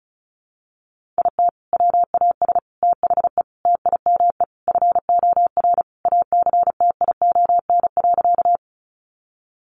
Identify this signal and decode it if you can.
Morse code, heard as a keyed tone, about 34 wpm, 700 Hz. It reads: IT WAS THE TIME FOR ACTION.
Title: Sentences from Top 700 Words 35wpm